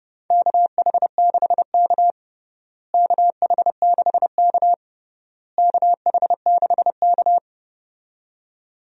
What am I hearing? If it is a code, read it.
Morse code, heard as a keyed tone, about 30 wpm, 700 Hz. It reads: KH6K KH6K KH6K
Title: DXCC-100 Call Signs - 4 Characters - Repeated 3x - 30wpm